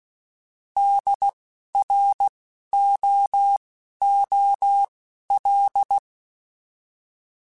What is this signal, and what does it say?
Morse code, heard as a keyed tone, about 16 wpm, spaced as for 12 wpm, 800 Hz. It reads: DROOL
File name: Alphabetical / Alphabetical mp3s / Ds @10